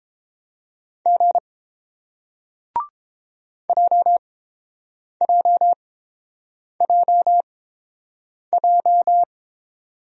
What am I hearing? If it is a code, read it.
Morse code, heard as a keyed tone, about 30 wpm, spaced as for 7 wpm, 700 Hz. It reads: G JJJJ